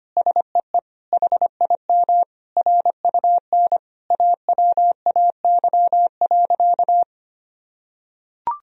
Morse code, heard as a keyed tone, about 25 wpm, 700 Hz. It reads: SEE HIM RUN AWAY.
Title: Sentences from Top 1000 Words - Review Part 1 - 25wpm